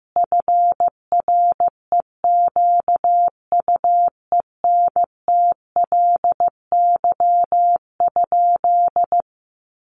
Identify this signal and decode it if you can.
Morse code, heard as a keyed tone, about 15 wpm, 700 Hz. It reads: FREQUENTLY?
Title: Sentences from Top 1300 Words 15wpm